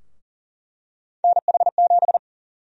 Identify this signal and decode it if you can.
Morse code, heard as a keyed tone, about 40 wpm, 700 Hz. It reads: NH7
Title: USA Call Sign Prefixes - 40wpm